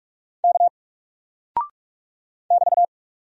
Morse code, heard as a keyed tone, about 44 wpm, 700 Hz. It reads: K <BT>
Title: Single Character (Letter:Number:Punctuation:Prosign) Alt 45wpm